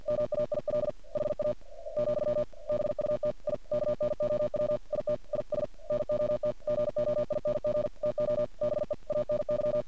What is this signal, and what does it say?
Morse code, heard as a keyed tone, about 51 wpm, 600 Hz. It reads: MAIL HA, BUT I KNOW IT IS NOT GOING TO BE ANY
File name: archived / CRSnet